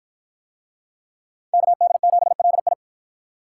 Morse code, heard as a keyed tone, about 53 wpm, 700 Hz. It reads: KD7LI